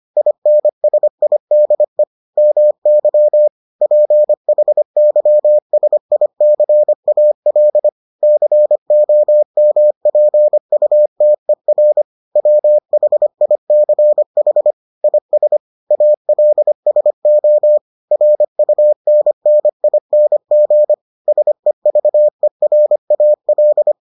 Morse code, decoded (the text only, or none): INSIDE MY PHYSICAL COMPUTER WHICH IS ALSO RUNNING SEVERAL